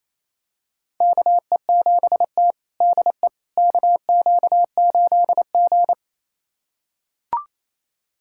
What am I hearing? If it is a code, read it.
Morse code, heard as a keyed tone, about 28 wpm, 700 Hz. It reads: KE7T DE KQ8G
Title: USA Callsign DE Callsign - 4 Characters - 28wpm